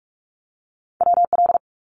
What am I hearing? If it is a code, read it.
Morse code, heard as a keyed tone, about 46 wpm, 700 Hz. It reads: WL
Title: Abbreviations 45wpm